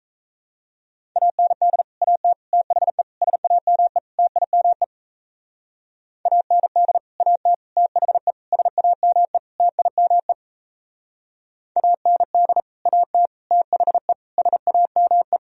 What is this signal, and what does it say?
Morse code, heard as a keyed tone, about 35 wpm, 700 Hz. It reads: AND AT THE SAME TIME AND AT THE SAME TIME AND AT THE SAME